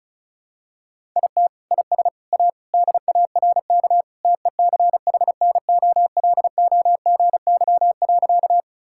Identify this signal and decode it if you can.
Morse code, heard as a keyed tone, about 36 wpm, 700 Hz. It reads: IT IS A DARK TECHNOLOGY.
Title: Sentences from Top 1000 Words - Review Part 5 - 35wpm